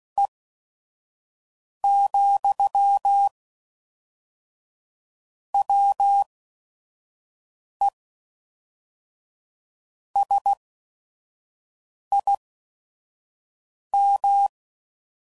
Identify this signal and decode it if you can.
Morse code, heard as a keyed tone, about 16 wpm, spaced as for 5 wpm, 800 Hz. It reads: E, WE SIM
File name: CW Code Groups @5WPM